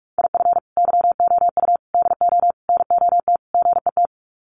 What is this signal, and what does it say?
Morse code, heard as a keyed tone, about 44 wpm, 700 Hz. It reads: IF YOU DO NOT GET
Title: Sets of 5 Words Encore2 45wpm